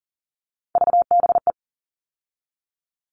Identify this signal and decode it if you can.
Morse code, heard as a keyed tone, about 40 wpm, 700 Hz. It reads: VBE
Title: All Cut Numbers - 3-Digit 40wpm